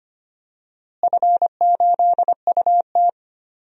Morse code, heard as a keyed tone, about 25 wpm, 700 Hz. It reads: F8UT
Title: DXCC-100 Call Signs - 4 Characters - Repeated 3x - 25wpm